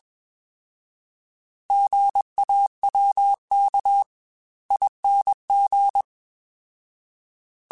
Morse code, heard as a keyed tone, about 21 wpm, spaced as for 18 wpm, 800 Hz. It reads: GAWKING
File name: Alphabetical / Alphabetical mp3s / Gs @20